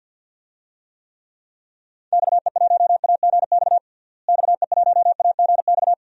Morse code, heard as a keyed tone, about 53 wpm, 700 Hz. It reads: XE1AGX XE1AGX
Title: DXCC-100 Call Signs - 6 Characters - Repeated 2x - 50wpm